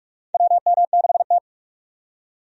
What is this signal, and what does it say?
Morse code, heard as a keyed tone, about 46 wpm, 700 Hz. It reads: WM6T
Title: USA Call Signs - 4 Characters - 45wpm